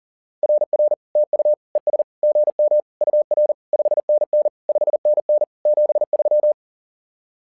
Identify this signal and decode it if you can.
Morse code, heard as a keyed tone, about 40 wpm, 600 Hz. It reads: RR TU ES GM UR 5NN 5NN 73
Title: SOTA QSO Practice 40wpm